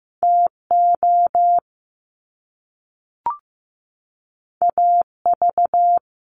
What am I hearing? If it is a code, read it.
Morse code, heard as a keyed tone, about 15 wpm, 700 Hz. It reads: TO AV
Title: Top Two Letter Combinations in Words 15wpm